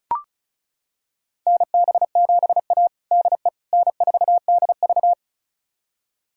Morse code, heard as a keyed tone, about 36 wpm, 700 Hz. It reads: NB7A DE N4DV
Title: USA Callsign DE Callsign - 4 Characters - 35wpm